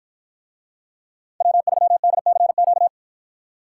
Morse code, heard as a keyed tone, about 51 wpm, 700 Hz. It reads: W3DCQ